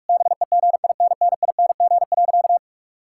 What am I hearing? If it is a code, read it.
Morse code, heard as a keyed tone, about 46 wpm, 700 Hz. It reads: BEGINNING.
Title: Sentences from Top 1000 Words - Review Part 5 - 45wpm